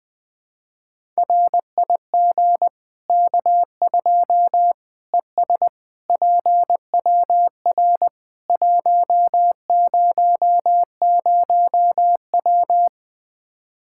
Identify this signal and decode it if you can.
Morse code, heard as a keyed tone, about 20 wpm, 700 Hz. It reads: RIG K2 ES PWR 100W
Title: K1USN SST QSO Practice 20wpm